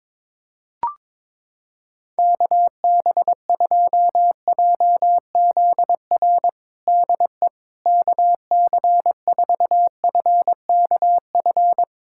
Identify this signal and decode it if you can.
Morse code, heard as a keyed tone, about 22 wpm, 700 Hz. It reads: KB2JZR DE KC4FKF
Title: USA Callsign DE Callsign - 6 Characters - 22wpm